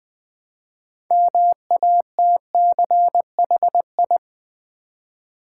Morse code, heard as a keyed tone, about 20 wpm, 700 Hz. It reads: MATCHI
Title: Build A Word - 2 Letters At A Time - 8-Letter Words - 20wpm